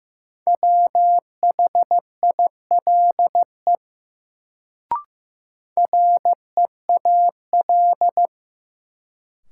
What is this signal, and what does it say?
Morse code, heard as a keyed tone, about 15 wpm, 700 Hz. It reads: WHILE REAL
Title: Sentences from Top 300 Words - New Word Drill - 15wpm